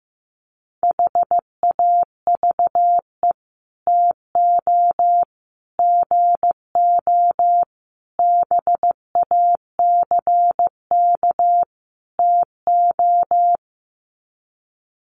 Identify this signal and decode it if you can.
Morse code, heard as a keyed tone, about 15 wpm, 700 Hz. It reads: HAVE TO GO BACK TO
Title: Sets of 5 Words Encore2 15wpm